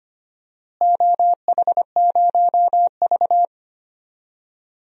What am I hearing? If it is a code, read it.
Morse code, heard as a keyed tone, about 25 wpm, 700 Hz. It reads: OH0V
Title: DXCC-100 Call Signs - 4 Characters - Repeated 2x - 25wpm